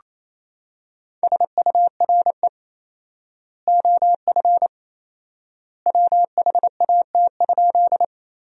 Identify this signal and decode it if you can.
Morse code, heard as a keyed tone, about 28 wpm, 700 Hz. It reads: SURE OF WHAT?
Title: 4x Word-Spacing - Sentences from Top 300 Words - 28wpm